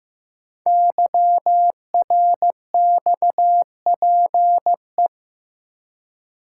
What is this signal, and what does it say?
Morse code, heard as a keyed tone, about 15 wpm, 700 Hz. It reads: YRXPE
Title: Code Groups - 5 Letters - 15wpm